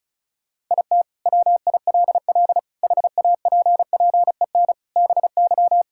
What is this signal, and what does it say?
Morse code, heard as a keyed tone, about 36 wpm, 700 Hz. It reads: IT WILL HAPPEN BY
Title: Sentences from Top 1000 Words - Review Part 5 - 35wpm